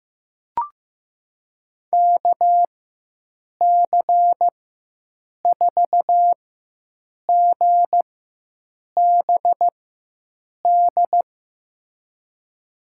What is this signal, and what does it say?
Morse code, heard as a keyed tone, about 15 wpm, spaced as for 7 wpm, 700 Hz. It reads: KC4GBD
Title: USA Call Signs - 6 Characters - 4x Character Spacing - 15wpm